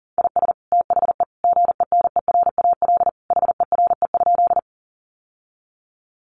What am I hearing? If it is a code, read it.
Morse code, heard as a keyed tone, about 40 wpm, 700 Hz. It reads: IS THE GENERAL HERE?
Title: Sentences from Top 600 Words 40wpm